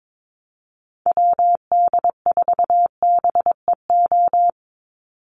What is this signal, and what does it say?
Morse code, heard as a keyed tone, about 22 wpm, 700 Hz. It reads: WD4BEO
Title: USA Call Signs 22wpm